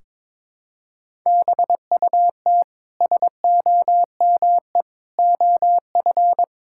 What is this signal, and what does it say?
Morse code, heard as a keyed tone, about 22 wpm, 700 Hz. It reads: BUT SOME OF
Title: Sentences from Top 1600 Words 22wpm